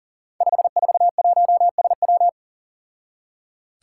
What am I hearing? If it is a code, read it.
Morse code, heard as a keyed tone, about 40 wpm, 700 Hz. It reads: 541SW
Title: Five Challenging Letter-Number Combinations 40wpm